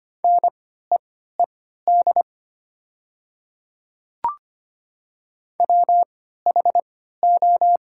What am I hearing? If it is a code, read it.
Morse code, heard as a keyed tone, about 25 wpm, spaced as for 14 wpm, 700 Hz. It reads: NEED WHO
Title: 3x Character Spacing - Top 300 Words - 25wpm